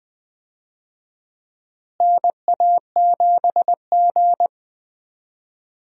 Morse code, heard as a keyed tone, about 20 wpm, 700 Hz. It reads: NA7G
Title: USA Call Signs - 4 Characters - Repeated 3x - 20wpm